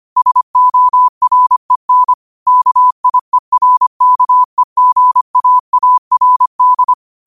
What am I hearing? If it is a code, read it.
Morse code, heard as a keyed tone, about 25 wpm, 1000 Hz. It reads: SOREN KIERKEGAARD